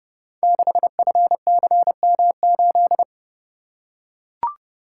Morse code, heard as a keyed tone, about 30 wpm, 700 Hz. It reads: 6FCM8
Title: Code Groups - 5 Letter:Numbers - 30wpm